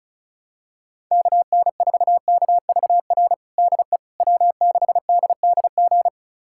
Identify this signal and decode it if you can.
Morse code, heard as a keyed tone, about 34 wpm, 700 Hz. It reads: KN4KVR DE W6DDG
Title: USA Callsign DE Callsign 35wpm